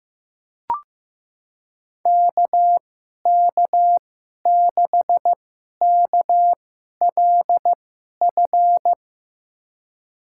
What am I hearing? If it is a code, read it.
Morse code, heard as a keyed tone, about 16 wpm, spaced as for 11 wpm, 700 Hz. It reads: KK6KLF
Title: USA Call Signs - 6 Characters - 2x Character Spacing - 15wpm